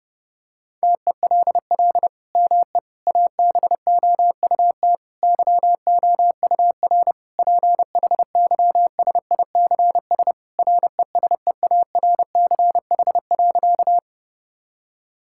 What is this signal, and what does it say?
Morse code, heard as a keyed tone, about 30 wpm, 700 Hz. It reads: TELL ME ABOUT YOUR PHYSICS RESEARCH.
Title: Sentences from Top 1000 Words - Review Part 5 - 30wpm